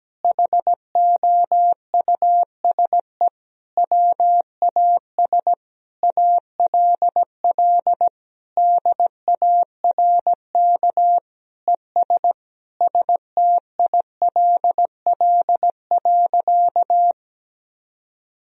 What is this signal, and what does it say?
Morse code, heard as a keyed tone, about 17 wpm, 700 Hz. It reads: HOUSE WAS ALL DARK ES STILL.
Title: Sentences from Top 500 Words and Abbreviations 17wpm